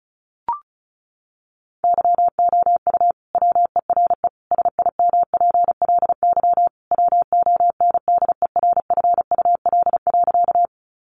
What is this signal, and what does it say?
Morse code, heard as a keyed tone, about 34 wpm, 700 Hz. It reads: YOU WERE SIMPLY WONDERFUL.
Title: Sentences from Top 1100 Words 35wpm